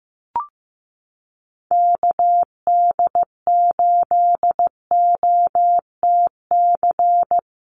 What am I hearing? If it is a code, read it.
Morse code, heard as a keyed tone, about 14 wpm, 700 Hz. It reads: KD8OTC
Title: USA Call Signs Repeated 3x 15wpm